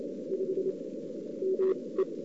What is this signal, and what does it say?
Morse code, heard as a keyed tone, about 27 wpm, spaced as for 20 wpm, 370 Hz. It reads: 5 ME
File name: CW / 2008-ss-cw-k5tr-op-n5rz -02:00